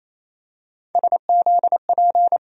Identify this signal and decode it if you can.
Morse code, heard as a keyed tone, about 28 wpm, 700 Hz. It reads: SZP
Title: USA Call Sign Postfixes - 3 Letters 28wpm